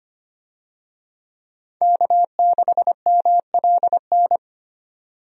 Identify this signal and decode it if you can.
Morse code, heard as a keyed tone, about 25 wpm, 700 Hz. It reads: K6MLN
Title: USA Call Signs - 5 Characters - 25wpm